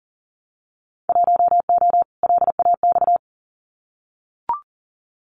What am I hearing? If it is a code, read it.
Morse code, heard as a keyed tone, about 38 wpm, 700 Hz. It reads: 1O LAX
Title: Winter Field Day Exchanges 40wpm